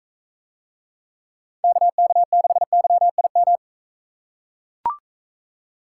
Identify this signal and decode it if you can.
Morse code, heard as a keyed tone, about 44 wpm, 700 Hz. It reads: KK6YIM